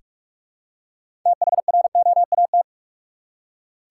Morse code, heard as a keyed tone, about 46 wpm, 700 Hz. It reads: THROAT
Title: Six Letter Words - 45wpm